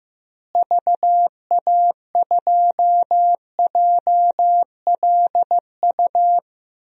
Morse code, heard as a keyed tone, about 15 wpm, 700 Hz. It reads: VA2JLU
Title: DXCC-100 Call Signs - 6 Characters - Repeated 3x - 15wpm